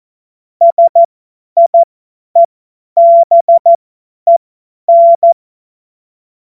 Morse code, heard as a keyed tone, about 14 wpm, spaced as for 10 wpm, 680 Hz. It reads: SIEBEN
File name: Morsekurs / Witze / Witz 21. Juni 2022 50 F70